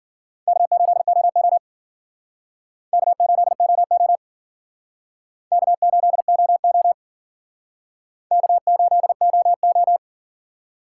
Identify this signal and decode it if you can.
Morse code, heard as a keyed tone, about 50 wpm, 700 Hz. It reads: K8OO K8OO K8OO K8OO
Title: Speed Racing - USA Call Signs 40wpm